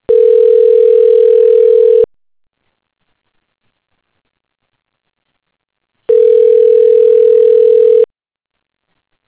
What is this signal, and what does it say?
Ringback tone (North America), 440 + 480 Hz.